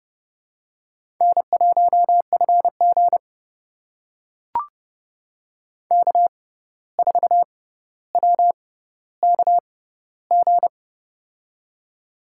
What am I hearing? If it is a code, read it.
Morse code, heard as a keyed tone, about 30 wpm, spaced as for 14 wpm, 700 Hz. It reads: N1FG K4WKG
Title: USA Call Signs - 6x Character Spacing - 30wpm